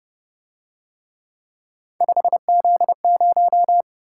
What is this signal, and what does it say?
Morse code, heard as a keyed tone, about 30 wpm, 700 Hz. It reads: 5Z0